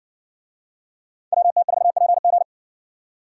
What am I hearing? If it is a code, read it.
Morse code, heard as a keyed tone, about 58 wpm, 700 Hz. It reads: WT4PG